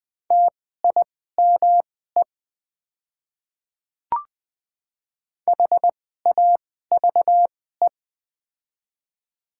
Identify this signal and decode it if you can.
Morse code, heard as a keyed tone, about 20 wpm, spaced as for 14 wpm, 700 Hz. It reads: TIME HAVE
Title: 2x Character Spacing - Top 100 Words - 20wpm